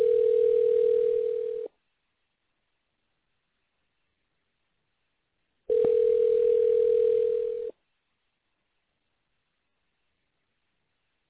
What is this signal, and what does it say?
Ringback tone (North America), 440 + 480 Hz.